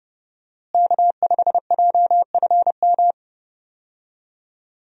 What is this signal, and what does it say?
Morse code, heard as a keyed tone, about 28 wpm, 700 Hz. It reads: K5JFM
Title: USA Call Signs - 5 Characters - Repeated 3x - 30wpm